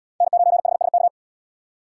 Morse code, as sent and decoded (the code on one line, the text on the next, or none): -. ----. .... ... -.-
N9HSK